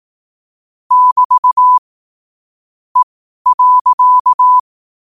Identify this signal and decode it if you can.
Morse code, heard as a keyed tone, about 18 wpm, spaced as for 12 wpm, 1000 Hz. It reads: <BT> E.
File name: qotdfiles / 12wpm